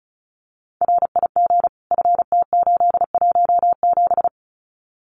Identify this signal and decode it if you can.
Morse code, heard as a keyed tone, about 34 wpm, 700 Hz. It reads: RIG FT817
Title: Common QSO Phrases Encore 35wpm